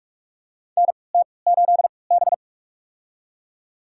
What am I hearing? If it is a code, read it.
Morse code, heard as a keyed tone, about 46 wpm, spaced as for 26 wpm, 700 Hz. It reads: NT8B
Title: USA Call Signs - 4 Characters - 3x Character Spacing - 45wpm